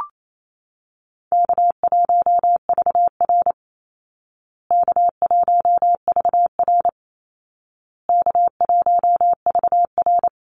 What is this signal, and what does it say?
Morse code, heard as a keyed tone, about 28 wpm, 700 Hz. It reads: K1VR K1VR K1VR
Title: Call Signs Repeated 2x - 28wpm